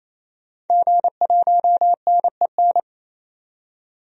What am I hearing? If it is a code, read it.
Morse code, heard as a keyed tone, about 28 wpm, 700 Hz. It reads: G1NEN
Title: DXCC-100 Call Signs - 5 Characters - 28wpm